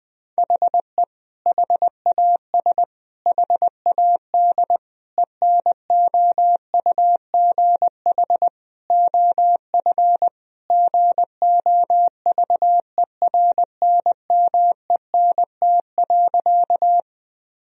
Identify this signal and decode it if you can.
Morse code, heard as a keyed tone, about 21 wpm, 700 Hz. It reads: HE HAS HAD ENOUGH OF GOVERNMENT.
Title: Sentences from Top 1000 Words - Review Part 3 - 20wpm